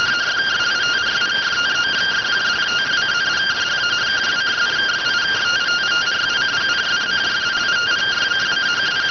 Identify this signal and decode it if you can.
Two-tone teleprinter signal, about 170 Hz shift.